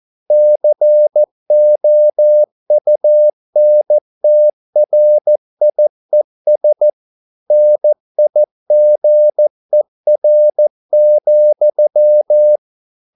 Morse code, heard as a keyed tone, about 14 wpm, 600 Hz. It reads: COUNTRIES NIGER,